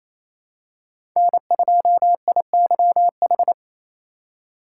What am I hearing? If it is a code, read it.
Morse code, heard as a keyed tone, about 27 wpm, 700 Hz. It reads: N2IYH
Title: USA Call Signs - 5 Characters - 4x Character Spacing - 28wpm